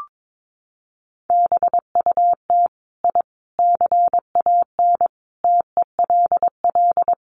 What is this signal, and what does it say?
Morse code, heard as a keyed tone, about 22 wpm, 700 Hz. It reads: BUT I CAN TELL
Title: Sets of 5 Words Encore 22wpm